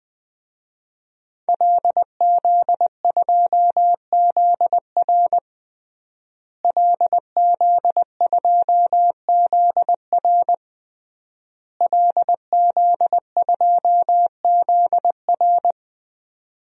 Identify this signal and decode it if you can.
Morse code, heard as a keyed tone, about 20 wpm, 700 Hz. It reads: LZ2ZR LZ2ZR LZ2ZR
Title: DXCC-100 Call Signs - 5 Characters - Repeated 3x - 20wpm